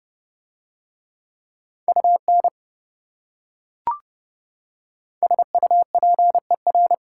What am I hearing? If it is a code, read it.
Morse code, heard as a keyed tone, about 28 wpm, 700 Hz. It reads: UN SUPER